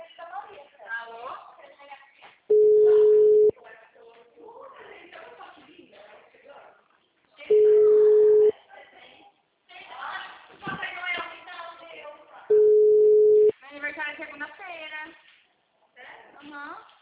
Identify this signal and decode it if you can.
Ringback tone (Europe, 425 Hz), with speech.